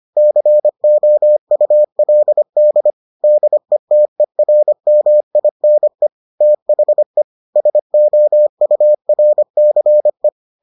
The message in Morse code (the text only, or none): COULD DETERMINE THE SOURCE